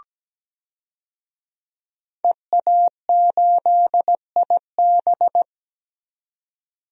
Morse code, heard as a keyed tone, about 17 wpm, 700 Hz. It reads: EA8IB
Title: DXCC-100 Call Signs - 5 Characters - 17wpm